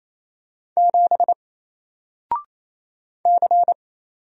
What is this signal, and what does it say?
Morse code, heard as a keyed tone, about 27 wpm, 700 Hz. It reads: 7 C
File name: Single-Letter-Number-28wpm